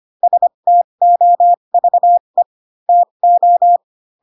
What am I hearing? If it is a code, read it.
Morse code, heard as a keyed tone, about 25 wpm, spaced as for 22 wpm, 700 Hz. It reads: STOVE TO
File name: Pinocchio / Pinocchio ch7 18 25 ik1hge